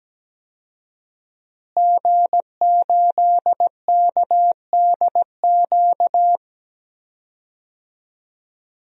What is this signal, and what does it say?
Morse code, heard as a keyed tone, about 17 wpm, 700 Hz. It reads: G8KDQ
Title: DXCC-100 Call Signs - 5 Characters - Repeated 2x - 17wpm